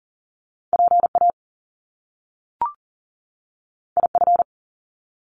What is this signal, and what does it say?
Morse code, heard as a keyed tone, about 40 wpm, 700 Hz. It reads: PA IF
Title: Top Two Letter Combinations in Words 40wpm